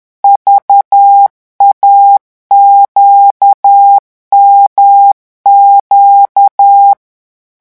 Morse code, heard as a keyed tone, about 10 wpm, 800 Hz. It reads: VAQMQ
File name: morse mp3 / revision3-vitesse 10 mots